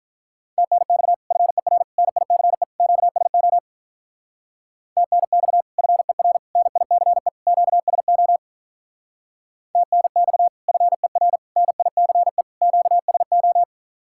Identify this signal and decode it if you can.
Morse code, heard as a keyed tone, about 46 wpm, 700 Hz. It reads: TNX FER NICE QSO TNX FER NICE QSO TNX FER NICE QSO